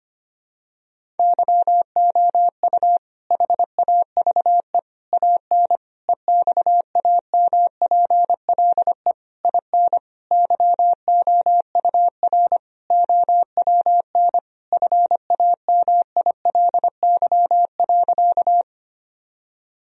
Morse code, heard as a keyed tone, about 25 wpm, 700 Hz. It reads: YOU HAVE AN EXAMPLE IN YOUR OWN FAMILY.
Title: Sentences from Top 1000 Words - Review Part 1 - 25wpm